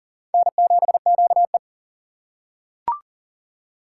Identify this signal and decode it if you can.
Morse code, heard as a keyed tone, about 40 wpm, 700 Hz. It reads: N7QE
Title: USA Call Signs - 4 Characters - 4x Character Spacing - 40wpm